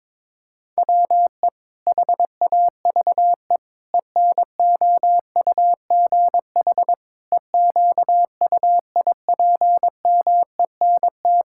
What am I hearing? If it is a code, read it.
Morse code, heard as a keyed tone, about 22 wpm, 700 Hz. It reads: WE HAVE ENOUGH EQUIPMENT
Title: Sentences from Top 1000 Words - Review Part 5 - 22wpm